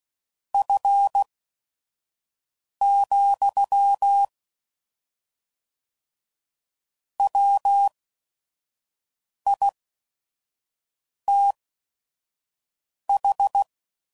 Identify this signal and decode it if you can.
Morse code, heard as a keyed tone, about 16 wpm, spaced as for 5 wpm, 800 Hz. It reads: F, WITH
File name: CW Code Groups @5WPM / Weather Sets / Weather 4 17